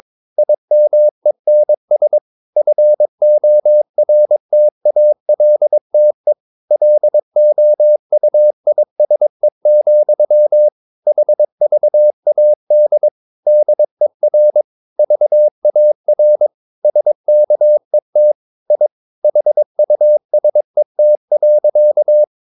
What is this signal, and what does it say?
Morse code, heard as a keyed tone, about 22 wpm, 600 Hz. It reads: IMENS FORTALTE LOUISE, HVAD DER VAR SKET I HUSET.
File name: Ferie QRS 22wpm